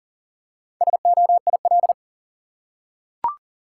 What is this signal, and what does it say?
Morse code, heard as a keyed tone, about 40 wpm, 700 Hz. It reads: SOIL
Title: Four Letter Words - 40wpm